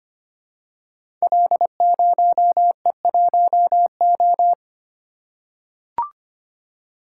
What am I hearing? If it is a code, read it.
Morse code, heard as a keyed tone, about 25 wpm, 700 Hz. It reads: L0E1O
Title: Code Groups - 5 Letter:Numbers - 25wpm